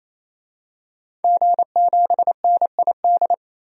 Morse code, heard as a keyed tone, about 27 wpm, 700 Hz. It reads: G7NID
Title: DXCC-100 Call Signs - 5 Characters - Repeated 3x - 28wpm